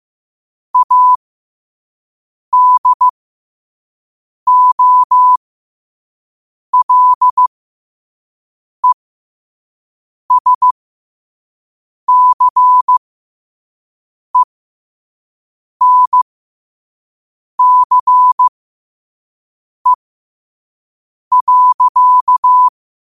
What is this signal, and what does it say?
Morse code, heard as a keyed tone, about 15 wpm, spaced as for 5 wpm, 1000 Hz. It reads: ADOLESCENCE.
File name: qotdfiles / 5wpm